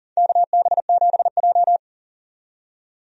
Morse code, heard as a keyed tone, about 40 wpm, 700 Hz. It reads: KB7J
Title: USA Call Signs - 4 Characters - 40wpm